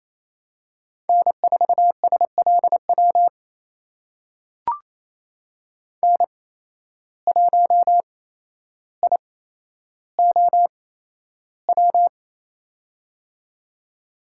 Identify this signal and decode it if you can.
Morse code, heard as a keyed tone, about 28 wpm, spaced as for 12 wpm, 700 Hz. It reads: N4SLW N1IOW
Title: USA Call Signs - 5 Characters - 8x Character Spacing - 28wpm